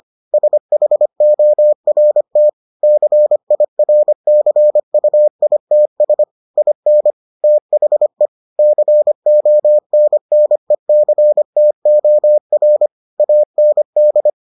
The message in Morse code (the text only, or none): SHORT CIRCUITS IN THE CONNECTOR AND